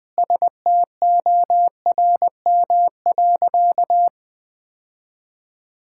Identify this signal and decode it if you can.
Morse code, heard as a keyed tone, about 20 wpm, 700 Hz. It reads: STORM.
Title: Sentences with 4-Word Phrases - Spoken Beginning - 20wpm